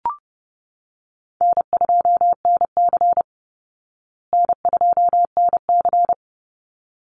Morse code, heard as a keyed tone, about 28 wpm, 700 Hz. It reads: N2NC N2NC
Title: Easy Call Signs Repeated 2x - 30wpm